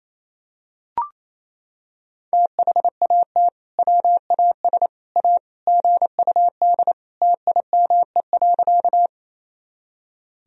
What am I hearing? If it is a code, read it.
Morse code, heard as a keyed tone, about 27 wpm, 700 Hz. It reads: THAT WAS A GUD TIME.
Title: Sentences from Top 200 Words and Abbreviations 28wpm